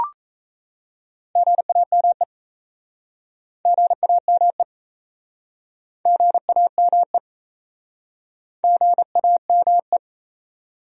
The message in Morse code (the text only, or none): GAME GAME GAME GAME